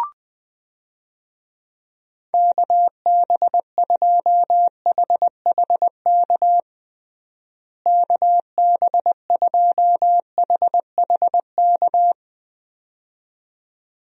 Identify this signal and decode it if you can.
Morse code, heard as a keyed tone, about 20 wpm, 700 Hz. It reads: KB2HHK KB2HHK
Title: USA Call Signs - 6 Characters - Repeated 2x - 20wpm